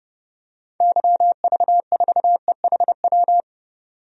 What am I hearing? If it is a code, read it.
Morse code, heard as a keyed tone, about 30 wpm, 700 Hz. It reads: YV4EHW
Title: DXCC-100 Call Signs - 6 Characters - 30wpm